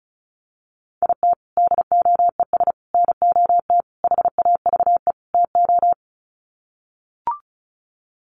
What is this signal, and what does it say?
Morse code, heard as a keyed tone, about 34 wpm, 700 Hz. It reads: IT DOES NOT HAVE TO
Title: Sets of 5 Words 35wpm